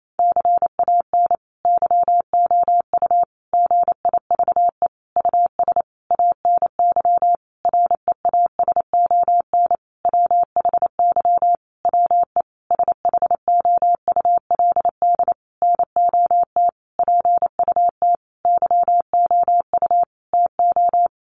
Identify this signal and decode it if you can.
Morse code, heard as a keyed tone, about 28 wpm, 700 Hz. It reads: CAN YOU GIVE US ANY REASON WHY WE SHOULD NOT PUT YOU TO
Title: Sentences from Top 1100 Words 28wpm